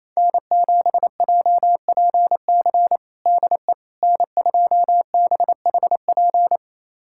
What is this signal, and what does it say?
Morse code, heard as a keyed tone, about 27 wpm, 690 Hz. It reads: N7JPC DE N2BHP
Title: USA Callsign DE Callsign - 5 Characters - 28wpm